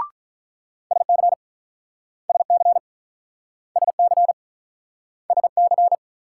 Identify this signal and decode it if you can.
Morse code, heard as a keyed tone, about 47 wpm, 700 Hz. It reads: SC SC SC SC